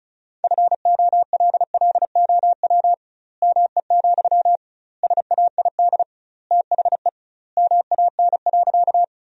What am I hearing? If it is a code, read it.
Morse code, heard as a keyed tone, about 36 wpm, 700 Hz. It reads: FOLLOW ME, SAID THE MAN.
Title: 2x Word-Spacing - Sentences from Top 200 Words - 35wpm